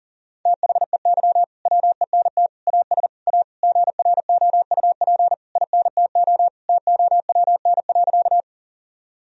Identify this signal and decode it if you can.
Morse code, heard as a keyed tone, about 40 wpm, 700 Hz. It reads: THEY WENT AS A GROUP INTO TOWN.
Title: Sentences from Top 1000 Words - Review Part 1 - 40wpm